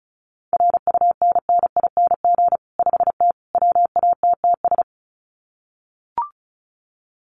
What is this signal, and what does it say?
Morse code, heard as a keyed tone, about 34 wpm, 700 Hz. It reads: RUNNING 5T WATTS
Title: Common QSO Phrases 35wpm